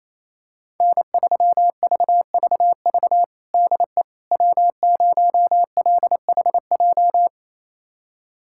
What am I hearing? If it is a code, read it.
Morse code, heard as a keyed tone, about 27 wpm, 700 Hz. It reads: N3VVV DE W0LHJ
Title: USA Callsign DE Callsign - 5 Characters - 28wpm